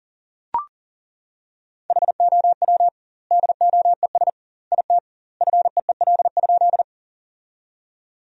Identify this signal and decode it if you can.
Morse code, heard as a keyed tone, about 37 wpm, 700 Hz. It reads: HOW DOES IT FEEL?
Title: 2x Word-Spacing - Sentences from Top 400 Words - 40wpm